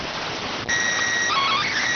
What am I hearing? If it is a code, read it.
SSTV (slow-scan television), PD 290.